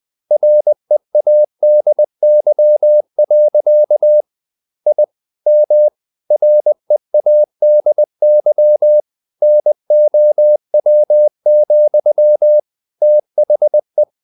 Morse code, heard as a keyed tone, about 20 wpm, 600 Hz. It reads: READY. I M READY NOW, THE
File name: the old man and the sea / the old man and the sea part 1 20wpm